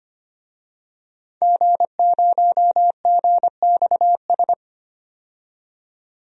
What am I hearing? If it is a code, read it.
Morse code, heard as a keyed tone, about 25 wpm, 700 Hz. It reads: G0GXS
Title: DXCC-100 Call Signs - 5 Characters - 25wpm